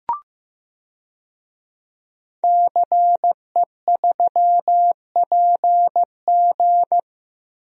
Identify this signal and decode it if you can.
Morse code, heard as a keyed tone, about 15 wpm, 700 Hz. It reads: CE3PG
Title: DXCC-100 Call Signs - 5 Characters - Repeated 2x - 15wpm